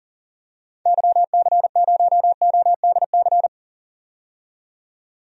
Morse code, heard as a keyed tone, about 40 wpm, 700 Hz. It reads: YC0ODC
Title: DXCC-100 Call Signs - 6 Characters - 40wpm